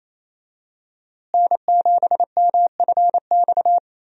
Morse code, heard as a keyed tone, about 28 wpm, 700 Hz. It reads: N7MFX
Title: USA Call Signs - 5 Characters - Repeated 2x - 28wpm